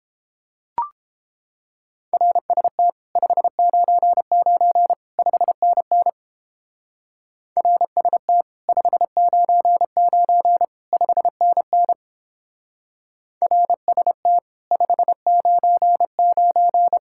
Morse code, heard as a keyed tone, about 29 wpm, 700 Hz. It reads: RST 599 5NN RST 599 5NN RST 599